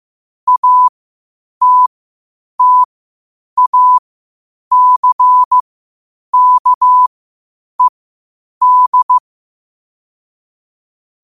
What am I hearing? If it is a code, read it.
Morse code, heard as a keyed tone, about 15 wpm, spaced as for 8 wpm, 1000 Hz. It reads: ATTACKED
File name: qotdfiles / 8wpm